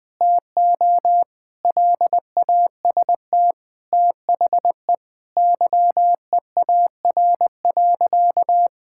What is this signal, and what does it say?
Morse code, heard as a keyed tone, about 20 wpm, 700 Hz. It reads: TO LAST THE YEAR.
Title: Sentences from Top 1600 Words 20wpm